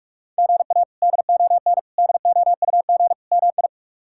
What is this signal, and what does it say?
Morse code, heard as a keyed tone, about 47 wpm, 700 Hz. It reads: GA DON DOUG MI
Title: K1USN SST QSO Practice - 45wpm